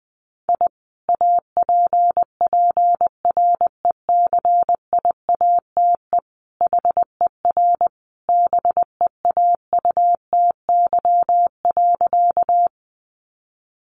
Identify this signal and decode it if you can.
Morse code, heard as a keyed tone, about 20 wpm, 700 Hz. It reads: I APPRECIATE HER BEAUTY.
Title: Sentences from Top 1300 Words 20wpm